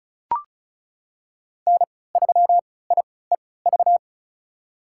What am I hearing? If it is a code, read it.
Morse code, heard as a keyed tone, about 36 wpm, spaced as for 20 wpm, 700 Hz. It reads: N3IEV
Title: USA Call Signs - 5 Characters - 3x Character Spacing - 35wpm